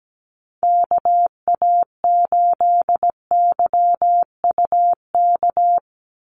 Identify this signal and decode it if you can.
Morse code, heard as a keyed tone, about 17 wpm, 700 Hz. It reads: KA8YUK
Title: USA Call Signs 17wpm